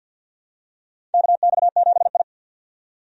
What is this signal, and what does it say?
Morse code, heard as a keyed tone, about 48 wpm, 700 Hz. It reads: KX7I
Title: USA Call Signs - 4 Characters - 50wpm